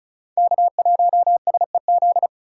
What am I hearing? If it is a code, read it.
Morse code, heard as a keyed tone, about 36 wpm, 700 Hz. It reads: K1SEZ
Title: USA Call Signs - 2x Character Spacing - 35wpm